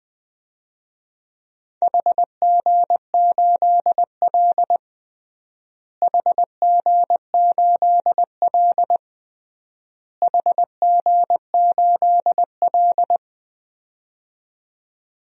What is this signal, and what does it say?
Morse code, heard as a keyed tone, about 20 wpm, 700 Hz. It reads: HG8L HG8L HG8L
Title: DXCC-100 Call Signs - 4 Characters - Repeated 3x - 20wpm